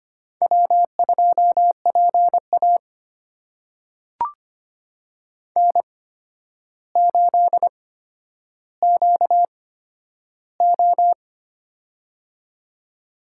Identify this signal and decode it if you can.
Morse code, heard as a keyed tone, about 25 wpm, spaced as for 11 wpm, 700 Hz. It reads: W2PA N8QO
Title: USA Call Signs - 4 Characters - 8x Character Spacing - 25wpm